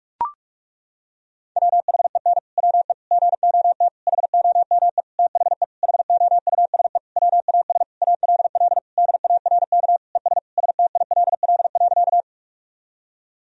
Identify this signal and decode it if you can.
Morse code, heard as a keyed tone, about 41 wpm, 700 Hz. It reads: WHEN WE GOT HOME THE HOUSE WAS ALL DARK ES STILL.
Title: Sentences from Top 500 Words and Abbreviations 45wpm